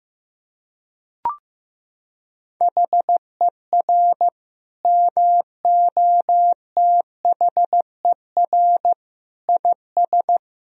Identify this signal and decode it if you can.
Morse code, heard as a keyed tone, about 15 wpm, 700 Hz. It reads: HER MOTHER IS
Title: Sentences from Top 1000 Words - Review Part 3 - 15wpm